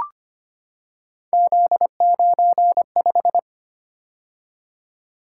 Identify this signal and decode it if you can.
Morse code, heard as a keyed tone, about 25 wpm, 700 Hz. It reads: Z95
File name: Three Letter-Number Combinations - 25wpm